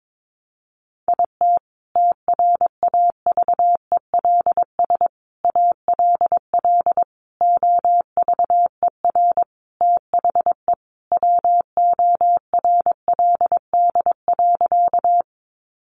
Morse code, heard as a keyed tone, about 22 wpm, 700 Hz. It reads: IT TRAVELS ALL OVER THE WORLD.
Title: Additional Easy Phrases 22wpm